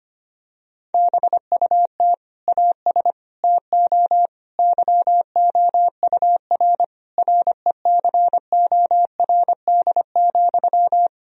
Morse code, heard as a keyed tone, about 25 wpm, 700 Hz. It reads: BUT AS TO YOUR RECORD,
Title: Sentences from Top 1000 Words - Review Part 3 - 25wpm